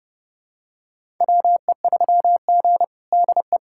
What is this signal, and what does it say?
Morse code, heard as a keyed tone, about 30 wpm, 700 Hz. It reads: WE3G DE
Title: USA Callsign DE Callsign - 4 Characters - 30wpm